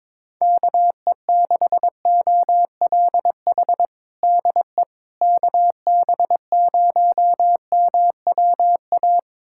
Morse code, heard as a keyed tone, about 22 wpm, 700 Hz. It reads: KE6OLH DE KB0MWA
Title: USA Callsign DE Callsign - 6 Characters - 22wpm